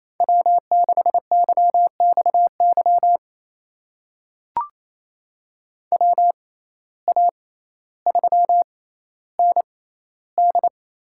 Morse code, heard as a keyed tone, about 28 wpm, spaced as for 14 wpm, 700 Hz. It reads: W6YXY WA3ND
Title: USA Call Signs - 6x Character Spacing - 28wpm